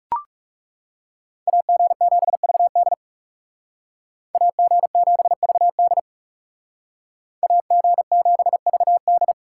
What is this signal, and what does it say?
Morse code, heard as a keyed tone, about 40 wpm, 700 Hz. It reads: AG7VD AG7VD AG7VD